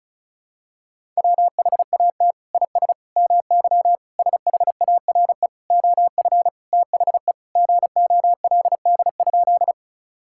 Morse code, heard as a keyed tone, about 34 wpm, 690 Hz. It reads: WHAT IS MY SHARE OF THE GOLD?
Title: Sentences from Top 1000 Words - Review Part 6 - 35wpm